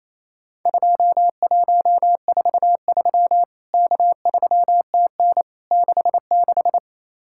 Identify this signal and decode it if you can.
Morse code, heard as a keyed tone, about 28 wpm, 700 Hz. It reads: 2143 K3TN 66
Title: Worked All Europe DX Contest - QTC Exchange - 28wpm